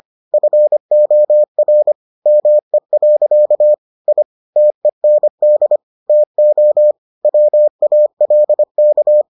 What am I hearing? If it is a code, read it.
Morse code, heard as a keyed tone, about 25 wpm, 600 Hz. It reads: FOR ME. I TEND TO WALK